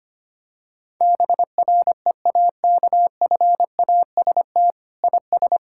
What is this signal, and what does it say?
Morse code, heard as a keyed tone, about 25 wpm, 700 Hz. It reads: BREAKFAST IS
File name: practice / 25wpm / Sentences with 9-Word Phrases - 25wpm